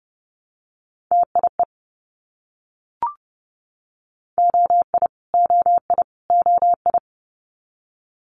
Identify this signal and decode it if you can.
Morse code, heard as a keyed tone, about 30 wpm, 700 Hz. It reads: TIE OI OI OI
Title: Multi-letter Phonemes Repeated 3x 30wpm